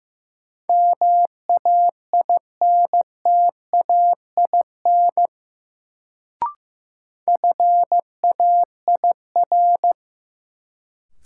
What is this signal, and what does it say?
Morse code, heard as a keyed tone, about 15 wpm, 700 Hz. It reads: MAINTAIN FAIR
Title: Sentences from Top 1000 Words - Review Part 5 - New Word Drill - 15wpm